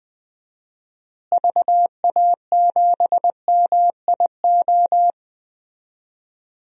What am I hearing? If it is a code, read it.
Morse code, heard as a keyed tone, about 21 wpm, 700 Hz. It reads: VA7MIO
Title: DXCC-100 Call Signs - 6 Characters - Repeated 3x - 20wpm